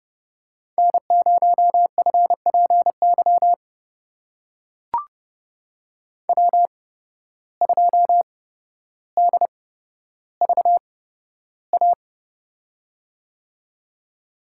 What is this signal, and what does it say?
Morse code, heard as a keyed tone, about 32 wpm, spaced as for 13 wpm, 700 Hz. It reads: N0FPY W2DVA
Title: USA Call Signs - 5 Characters - 8x Character Spacing - 30wpm